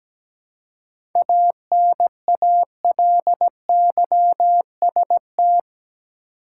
Morse code, heard as a keyed tone, about 17 wpm, 700 Hz. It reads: ANALYST
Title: Seven Letter Words - 17wpm